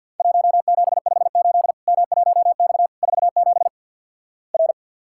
Morse code, heard as a keyed tone, about 50 wpm, 700 Hz. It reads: 1758 M1X 47 R
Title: Worked All Europe DX Contest - QTC Exchange with Repeats - 50wpm